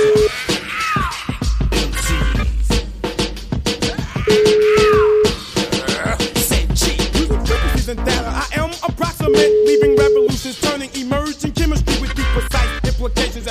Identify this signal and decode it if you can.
Ringback tone (Europe, 425 Hz), with noise.